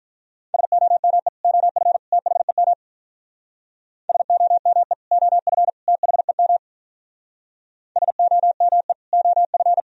Morse code, heard as a keyed tone, about 42 wpm, 700 Hz. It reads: SOME OF THEM SOME OF THEM SOME OF
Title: Speed-Racing - Sets of 3 Words 35wpm